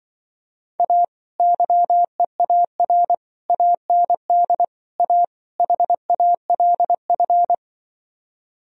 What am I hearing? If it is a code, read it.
Morse code, heard as a keyed tone, about 24 wpm, 700 Hz. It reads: A YEAR AND A HALF
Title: Speed-Racing - Sets of 5 Words 20wpm